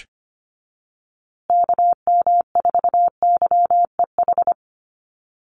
Morse code, heard as a keyed tone, about 25 wpm, 700 Hz. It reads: KM4YEH
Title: USA Call Signs Repeated 2x 25wpm